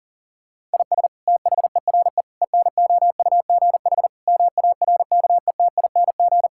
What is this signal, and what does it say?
Morse code, heard as a keyed tone, about 40 wpm, 700 Hz. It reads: IS THERE ENOUGH MARKETING
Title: Sentences from Top 1000 Words - Review Part 5 - 40wpm